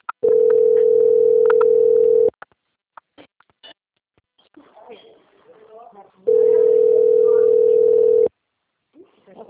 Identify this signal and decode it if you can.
Ringback tone (North America), 440 + 480 Hz, with speech.
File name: out- - - - - .4151